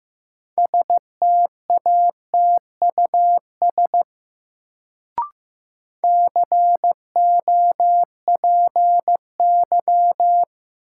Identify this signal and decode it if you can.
Morse code, heard as a keyed tone, about 15 wpm, 700 Hz. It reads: STATUS COPY
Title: Sentences from Top 1400 Words - New Word Drill Alt - 15wpm